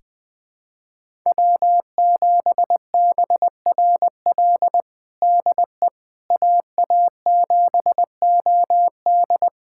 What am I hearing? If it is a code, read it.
Morse code, heard as a keyed tone, about 20 wpm, 700 Hz. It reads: W7BRL DE AA7OD
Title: USA Callsign DE Callsign - 5 Characters - 20wpm